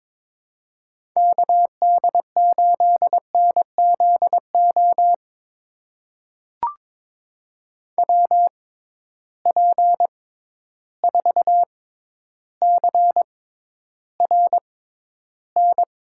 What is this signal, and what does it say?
Morse code, heard as a keyed tone, about 22 wpm, spaced as for 11 wpm, 700 Hz. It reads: KD8NZO WP4CRN
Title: USA Call Signs - 6 Characters - 6x Character Spacing - 22wpm